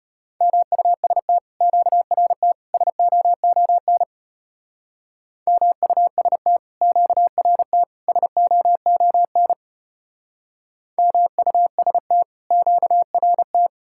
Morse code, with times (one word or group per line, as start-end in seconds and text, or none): 0.4-1.4 MUST
1.6-2.5 QRT
2.7-4.0 SOON
5.5-6.6 MUST
6.8-7.8 QRT
8.1-9.5 SOON
11.0-12.2 MUST
12.5-13.7 QRT